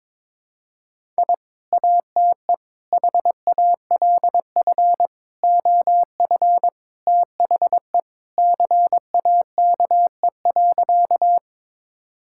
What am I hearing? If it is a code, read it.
Morse code, heard as a keyed tone, about 22 wpm, 700 Hz. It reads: I ATE HALF OF THE CAKE.
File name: practice / 22wpm / Sentences with 3-Word Phrases - 22wpm